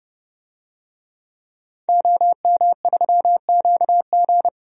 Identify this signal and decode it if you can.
Morse code, heard as a keyed tone, about 31 wpm, 700 Hz. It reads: OM3QG
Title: DXCC-100 Call Signs - 5 Characters - 30wpm